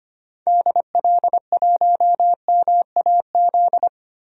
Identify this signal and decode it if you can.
Morse code, heard as a keyed tone, about 25 wpm, 700 Hz. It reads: DL1MAZ
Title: DXCC-100 Call Signs - 6 Characters - 25wpm